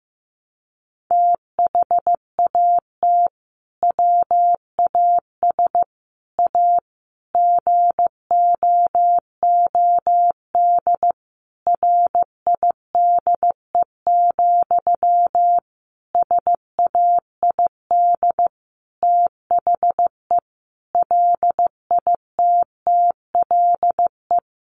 Morse code, heard as a keyed tone, about 15 wpm, 700 Hz. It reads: THAT WAS A GOOD RIDE, SAID THE LITTLE
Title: Sentences from Top 1200 Words 15wpm